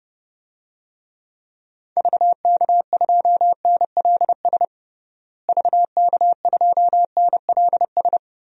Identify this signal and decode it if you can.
Morse code, heard as a keyed tone, about 30 wpm, 700 Hz. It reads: VK2NLS VK2NLS
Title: DXCC-100 Call Signs - 6 Characters - Repeated 3x - 30wpm